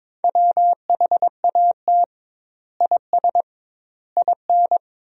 Morse code, heard as a keyed tone, about 22 wpm, 700 Hz. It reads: WHAT IS IN
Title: 2x Word-Spacing - Sentences from Top 200 Words - 22wpm